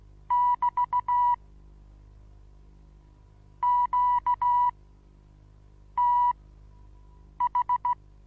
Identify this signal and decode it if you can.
Morse code, heard as a keyed tone, about 16 wpm, spaced as for 6 wpm, 980 Hz. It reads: <BT> QTH
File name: Onlinelehrgang / morsen / DJ4UF Morsen25